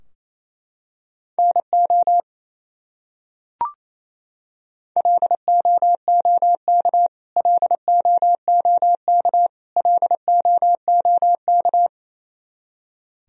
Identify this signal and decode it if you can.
Morse code, heard as a keyed tone, about 28 wpm, 700 Hz. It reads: NO LOOK LOOK LOOK
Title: Top 100 Words Repeated 28wpm